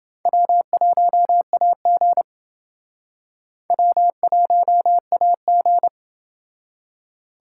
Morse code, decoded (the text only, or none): W1AG W1AG